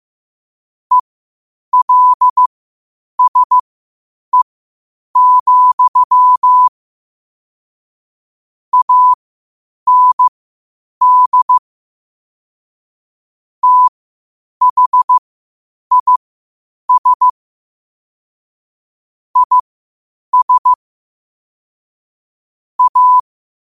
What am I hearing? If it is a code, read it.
Morse code, heard as a keyed tone, about 15 wpm, spaced as for 8 wpm, 1000 Hz. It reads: ELSE, AND THIS IS A
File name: qotdfiles / 8wpm